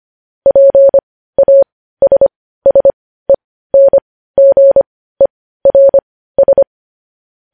Morse code, heard as a keyed tone, about 25 wpm, spaced as for 15 wpm, 550 Hz. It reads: PASSENGERS
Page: news headlines in morse code at 15 wpm